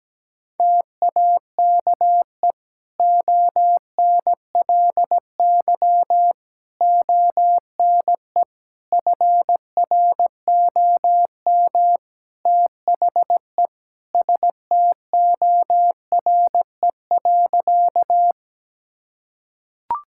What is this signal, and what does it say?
Morse code, heard as a keyed tone, about 17 wpm, 700 Hz. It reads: TAKE ONLY ONE FROM THE STORE.
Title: Sentences from Top 1000 Words - Review Part 5 - 17wpm